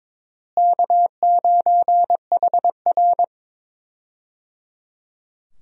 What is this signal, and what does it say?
Morse code, heard as a keyed tone, about 22 wpm, 700 Hz. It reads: K9HR
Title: USA Call Signs - 4 Characters - Repeated 3x - 22wpm